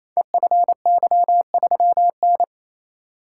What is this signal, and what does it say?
Morse code, heard as a keyed tone, about 28 wpm, 700 Hz. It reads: EFY3N
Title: Code Groups - 5 Letter:Numbers - 28wpm